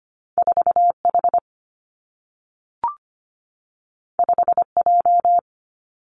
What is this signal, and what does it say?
Morse code, heard as a keyed tone, about 25 wpm, 700 Hz. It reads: 4 H 5 J
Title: Two Challenging Letter-Number Combinations 25wpm